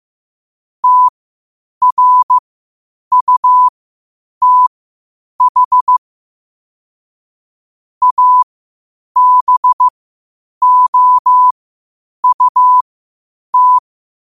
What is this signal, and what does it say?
Morse code, heard as a keyed tone, about 15 wpm, spaced as for 8 wpm, 1000 Hz. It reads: TRUTH ABOUT
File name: qotdfiles / 8wpm